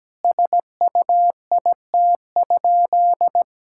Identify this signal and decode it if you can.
Morse code, heard as a keyed tone, about 17 wpm, 700 Hz. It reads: SUIT?
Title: Sentences from Top 1600 Words 17wpm